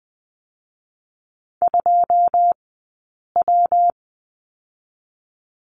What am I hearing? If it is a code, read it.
Morse code, heard as a keyed tone, about 20 wpm, 700 Hz. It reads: 2 W
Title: 137 Lesson 23d - Letter f - Two Characters 20wpm